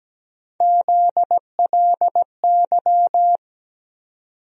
Morse code, heard as a keyed tone, about 17 wpm, 700 Hz. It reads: ZLY
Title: USA Call Sign Postfixes - 3 Letters 17wpm